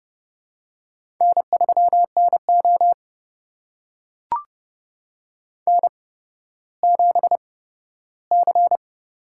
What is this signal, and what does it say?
Morse code, heard as a keyed tone, about 30 wpm, spaced as for 15 wpm, 700 Hz. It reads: N3NO N7C
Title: USA Call Signs - 4 Characters - 8x Character Spacing - 30wpm